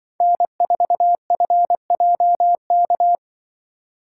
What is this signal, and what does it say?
Morse code, heard as a keyed tone, about 24 wpm, 700 Hz. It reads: N4FJK
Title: Speed Racing - USA 1x3 Call Signs 20wpm